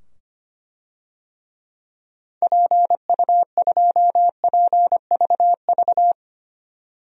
Morse code, heard as a keyed tone, about 25 wpm, 700 Hz. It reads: PU2PVV
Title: DXCC-100 Call Signs - 6 Characters - 25wpm